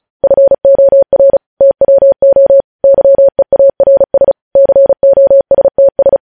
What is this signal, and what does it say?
Morse code, heard as a keyed tone, about 36 wpm, 550 Hz. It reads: FOR TWO YEARS COSTS
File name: podcast / News headlines-1x-35x35